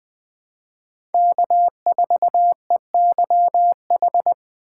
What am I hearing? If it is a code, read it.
Morse code, heard as a keyed tone, about 20 wpm, 700 Hz. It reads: K4EYH
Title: USA Call Signs - 5 Characters - 3x Character Spacing - 20wpm